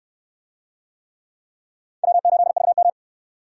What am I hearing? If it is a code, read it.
Morse code, heard as a keyed tone, about 62 wpm, 700 Hz. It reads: W8VM